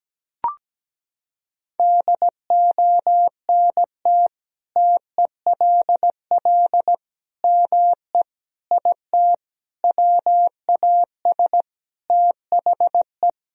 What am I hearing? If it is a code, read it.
Morse code, heard as a keyed tone, about 17 wpm, 700 Hz. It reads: DONT TELL ME IT WAS THE
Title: Sentences from Top 1000 Words - Review Part 3 - 17wpm